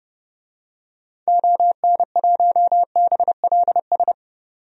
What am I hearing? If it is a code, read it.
Morse code, heard as a keyed tone, about 30 wpm, 700 Hz. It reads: ON1BLS
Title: DXCC-100 Call Signs - 6 Characters - 30wpm